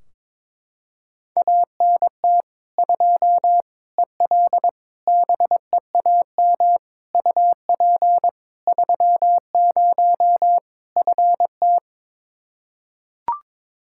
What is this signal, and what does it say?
Morse code, heard as a keyed tone, about 21 wpm, 700 Hz. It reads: ANT 2 EL BEAM UP 30 FT
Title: Practice QSO - Exchange 2 - Info - 22wpm